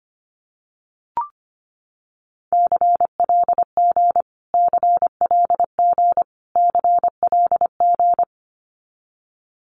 Morse code, heard as a keyed tone, about 25 wpm, 700 Hz. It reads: CLG CLG CLG
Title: Abbreviations Repeated 3x - 25wpm